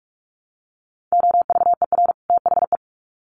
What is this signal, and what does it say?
Morse code, heard as a keyed tone, about 44 wpm, 700 Hz. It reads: OVER THE
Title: Sets of 2 Words 45wpm